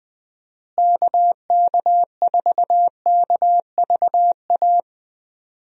Morse code, heard as a keyed tone, about 20 wpm, 700 Hz. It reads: KK4KVA
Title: USA Call Signs - 6 Characters - Repeated 2x - 20wpm